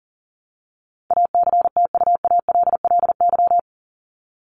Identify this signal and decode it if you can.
Morse code, heard as a keyed tone, about 40 wpm, 700 Hz. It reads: ACTUALLY
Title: Top 501-750 Words 40wpm